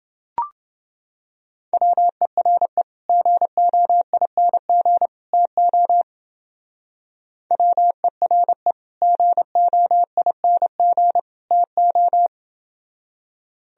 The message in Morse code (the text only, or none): WERE GOING TO WERE GOING TO